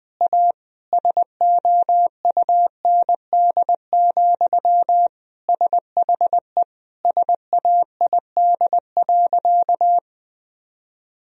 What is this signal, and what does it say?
Morse code, heard as a keyed tone, about 20 wpm, 700 Hz. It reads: A SOUND, SHE SAID.
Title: Sentences from Top 1000 Words - Review Part 1 - 20wpm